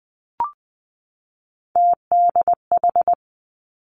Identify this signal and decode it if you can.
Morse code, heard as a keyed tone, about 20 wpm, 700 Hz. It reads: TDH
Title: Three Challenging Letter-Number Combinations 20wpm